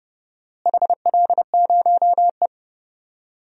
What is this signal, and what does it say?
Morse code, heard as a keyed tone, about 30 wpm, 700 Hz. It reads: HL0E
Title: DXCC-100 Call Signs - 4 Characters - Repeated 2x - 30wpm